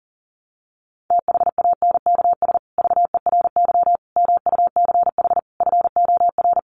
Morse code, heard as a keyed tone, about 40 wpm, 700 Hz. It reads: THANKS VERY MUCH FOR
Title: Sets of 5 Words Encore2 40wpm